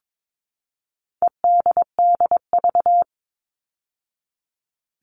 Morse code, heard as a keyed tone, about 22 wpm, 700 Hz. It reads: EDDV
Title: All Cut Numbers - 4-Digit 22wpm